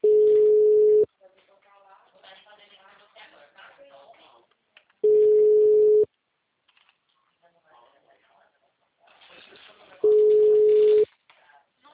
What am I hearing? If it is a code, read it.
Ringback tone (Europe, 425 Hz), with speech.
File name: out- -2012- - - .27171